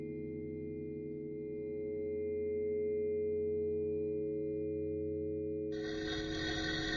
Dial tone (Europe, 425 Hz), with noise.